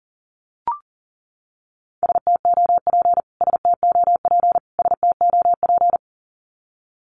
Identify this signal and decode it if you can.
Morse code, heard as a keyed tone, about 40 wpm, 700 Hz. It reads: STOP STOP STOP
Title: Top 500 Words Repeated 40wpm